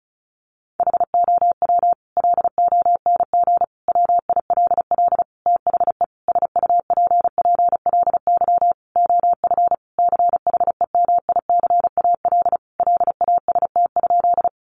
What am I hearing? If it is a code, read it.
Morse code, heard as a keyed tone, about 34 wpm, 700 Hz. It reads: HOW LONG WILL THE SUPPLY OF CHEMICAL LAST?
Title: Sentences from Top 1200 Words 35wpm